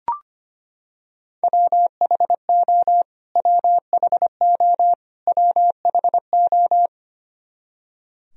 Morse code, heard as a keyed tone, about 25 wpm, 700 Hz. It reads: WHO WHO WHO
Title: Top 100 Words Repeated 25wpm